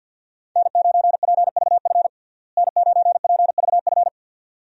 Morse code, heard as a keyed tone, about 46 wpm, 700 Hz. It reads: N9PVF N9PVF
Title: USA Call Signs - 5 Characters - Repeated 2x - 50wpm